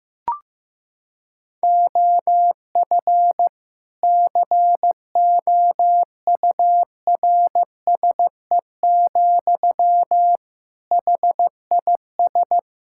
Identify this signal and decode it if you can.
Morse code, heard as a keyed tone, about 15 wpm, 700 Hz. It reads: OF COURSE, HIS
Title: Sentences from Top 1000 Words - Review Part 5 - 15wpm